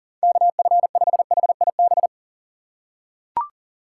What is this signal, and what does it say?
Morse code, heard as a keyed tone, about 40 wpm, 700 Hz. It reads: KF5HIB
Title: USA Call Signs - 6 Characters - 2x Character Spacing - 40wpm